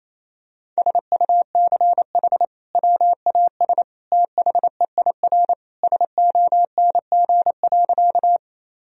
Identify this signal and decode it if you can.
Morse code, heard as a keyed tone, about 28 wpm, 700 Hz. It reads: SUCH WAS THEIR SONG.
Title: Sentences from Top 1000 Words - Review Part 2 - 28wpm